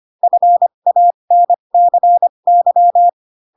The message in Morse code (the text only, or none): FANCY